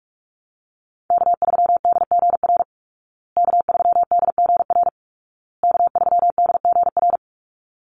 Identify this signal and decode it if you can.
Morse code, heard as a keyed tone, about 44 wpm, 700 Hz. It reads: K3DGR K3DGR K3DGR
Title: Call Signs Repeated 2x - 45wpm